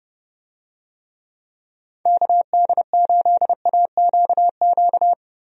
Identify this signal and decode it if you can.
Morse code, heard as a keyed tone, about 31 wpm, 700 Hz. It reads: KD8AQQ
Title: USA Call Signs - 6 Characters - 30wpm